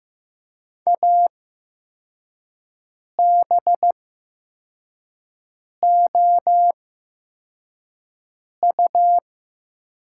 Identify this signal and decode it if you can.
Morse code, heard as a keyed tone, about 15 wpm, spaced as for 4 wpm, 700 Hz. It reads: ABOU
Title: 8x Character Spacing - Top 100 Words - 15wpm